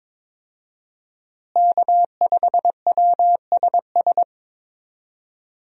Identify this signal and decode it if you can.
Morse code, heard as a keyed tone, about 22 wpm, 700 Hz. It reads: K5WSS
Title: USA Call Signs - 5 Characters - Repeated 2x - 22wpm